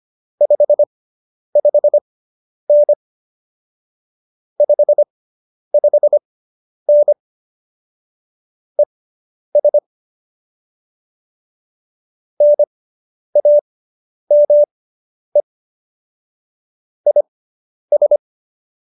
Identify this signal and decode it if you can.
Morse code, heard as a keyed tone, about 25 wpm, spaced as for 10 wpm, 590 Hz. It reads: 55N 55N ES NAME IS